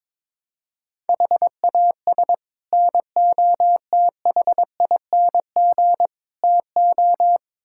Morse code, heard as a keyed tone, about 22 wpm, 700 Hz. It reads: HAS NOTHING TO
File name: practice / 22wpm / Build A Phrase - 5-Word Phrases - 22wpm